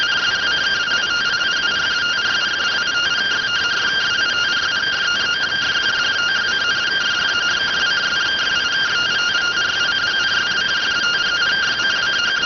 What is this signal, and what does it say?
Two-tone teleprinter signal, about 170 Hz shift.